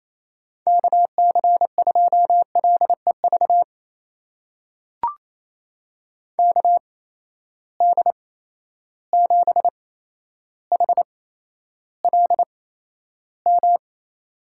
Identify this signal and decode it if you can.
Morse code, heard as a keyed tone, about 28 wpm, spaced as for 12 wpm, 700 Hz. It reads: KC2LEV KD7HLM
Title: USA Call Signs - 6 Characters - 8x Character Spacing - 28wpm